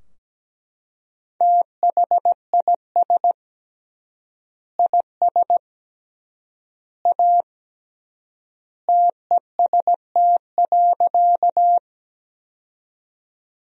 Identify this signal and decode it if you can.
Morse code, heard as a keyed tone, about 17 wpm, 700 Hz. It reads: THIS IS A TEST.
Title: 3x Word-Spacing - Sentences from Top 500 Words - 17wpm